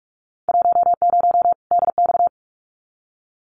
Morse code, heard as a keyed tone, about 46 wpm, 700 Hz. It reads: 10 DX
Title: Winter Field Day Exchanges 45wpm